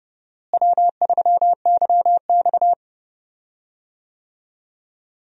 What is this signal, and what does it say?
Morse code, heard as a keyed tone, about 32 wpm, 700 Hz. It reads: W3YX
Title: USA Call Signs - 4 Characters - Repeated 3x - 30wpm